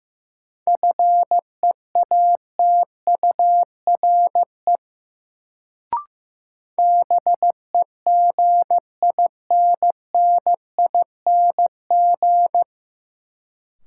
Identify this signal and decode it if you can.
Morse code, heard as a keyed tone, about 16 wpm, 700 Hz. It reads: FEATURE BEGINNING
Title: Sentences from Top 800 Words - New Word Drill - 15wpm